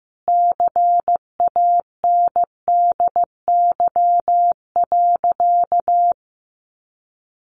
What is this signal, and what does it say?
Morse code, heard as a keyed tone, about 15 wpm, 700 Hz. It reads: CANDY.
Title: Easy Phrases 15wpm